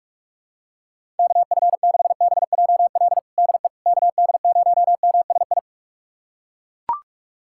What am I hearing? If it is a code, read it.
Morse code, heard as a keyed tone, about 41 wpm, 700 Hz. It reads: KF6BJL DE KD0MSI
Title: USA Callsign DE Callsign - 6 Characters - 45wpm